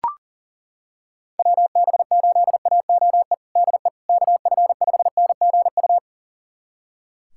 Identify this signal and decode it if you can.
Morse code, heard as a keyed tone, about 37 wpm, 700 Hz. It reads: WB8AOE DE KF5NGU
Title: USA Callsign DE Callsign - 6 Characters - 40wpm